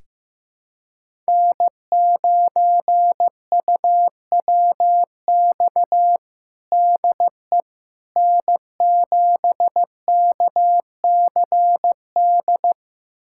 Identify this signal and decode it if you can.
Morse code, heard as a keyed tone, about 16 wpm, 700 Hz. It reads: N9UWX DE N7KCD
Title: USA Callsign DE Callsign - 5 Characters - 15wpm